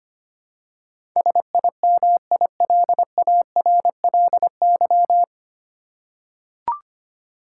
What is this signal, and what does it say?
Morse code, heard as a keyed tone, about 25 wpm, 700 Hz. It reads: SIMILARLY
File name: Nine Letter Words - 25wpm